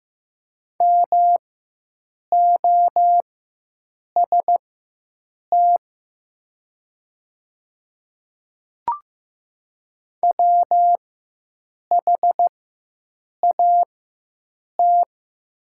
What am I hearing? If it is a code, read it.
Morse code, heard as a keyed tone, about 15 wpm, spaced as for 7 wpm, 700 Hz. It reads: MOST WHAT
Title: 4x Character Spacing - Top 100 Words - 15wpm